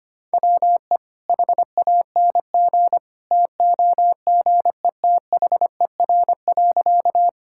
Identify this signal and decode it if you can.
Morse code, heard as a keyed tone, about 25 wpm, 700 Hz. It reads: WE HANG TOGETHER.
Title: Sentences from Top 1600 Words 25wpm